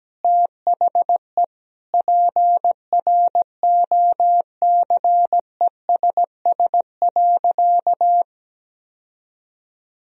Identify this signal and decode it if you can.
Morse code, heard as a keyed tone, about 17 wpm, 700 Hz. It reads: THE PROCESS.
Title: Sentences from Top 1000 Words - Review Part 3 - 17wpm